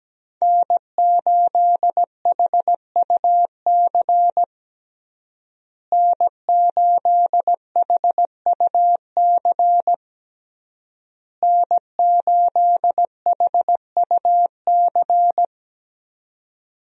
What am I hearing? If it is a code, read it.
Morse code, heard as a keyed tone, about 17 wpm, 700 Hz. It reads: N8HUC N8HUC N8HUC
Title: USA Call Signs - 5 Characters - Repeated 3x - 17wpm